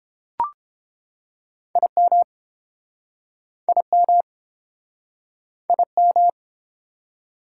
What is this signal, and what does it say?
Morse code, heard as a keyed tone, about 32 wpm, 700 Hz. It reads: IM IM IM